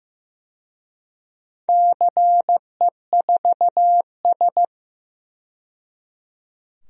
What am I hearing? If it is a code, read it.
Morse code, heard as a keyed tone, about 15 wpm, 700 Hz. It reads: CE4S
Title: DXCC-100 Call Signs - 4 Characters - 15wpm